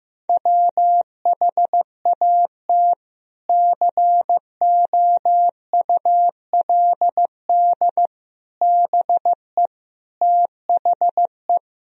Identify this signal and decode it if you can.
Morse code, heard as a keyed tone, about 15 wpm, 700 Hz. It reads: WHAT COULD BE THE
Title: Sentences from Top 1000 Words - Review Part 3 - 15wpm